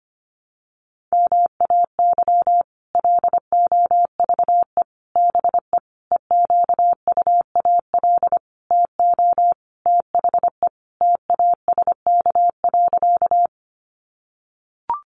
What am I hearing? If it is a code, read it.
Morse code, heard as a keyed tone, about 25 wpm, 700 Hz. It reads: MAY LOVE BE EQUAL TO THE TASK.
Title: Sentences from Top 1400 Words 25wpm